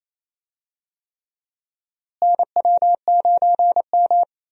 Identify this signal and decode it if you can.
Morse code, heard as a keyed tone, about 28 wpm, 700 Hz. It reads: NW9M
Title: USA Call Signs - 4 Characters - Repeated 3x - 28wpm